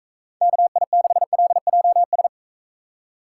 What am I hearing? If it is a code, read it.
Morse code, heard as a keyed tone, about 43 wpm, 700 Hz. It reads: KI6LJS